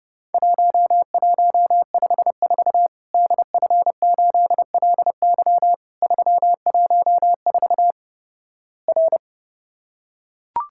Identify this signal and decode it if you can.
Morse code, heard as a keyed tone, about 30 wpm, 700 Hz. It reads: 1154 DF8LY 314 R
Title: Worked All Europe DX Contest - QTC Exchange - 30wpm